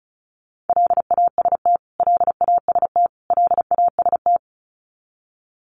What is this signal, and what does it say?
Morse code, heard as a keyed tone, about 34 wpm, 700 Hz. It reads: LAST LAST LAST
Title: Top 500 Words Repeated 35wpm